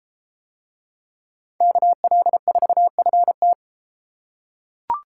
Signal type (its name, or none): Morse code